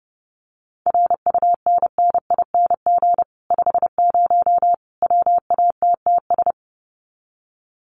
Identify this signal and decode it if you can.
Morse code, heard as a keyed tone, about 30 wpm, 700 Hz. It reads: RUNNING 50 WATTS
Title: Common QSO Phrases 30wpm